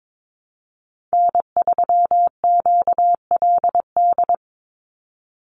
Morse code, heard as a keyed tone, about 22 wpm, 700 Hz. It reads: N3QLD
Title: USA Call Signs 22wpm